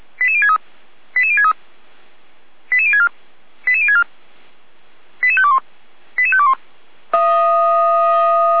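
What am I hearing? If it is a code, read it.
Five-tone selective call (ZVEI), heard as quick runs of five tones, tones of about 70 ms. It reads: 88063 88063 88064 88064 88521 88521
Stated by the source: ZVEI five-tone call 88365 received on SDR.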